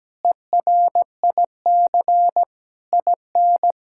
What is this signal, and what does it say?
Morse code, heard as a keyed tone, about 17 wpm, 700 Hz. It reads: ERIC IN
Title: K1USN SST QSO Practice - 17wpm